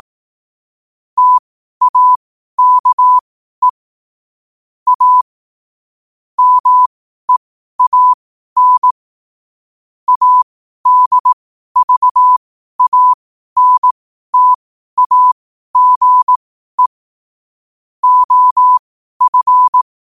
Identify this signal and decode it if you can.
Morse code, heard as a keyed tone, about 18 wpm, spaced as for 12 wpm, 1000 Hz. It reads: TAKE A MEAN ADVANTAGE OF
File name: qotdfiles / 12wpm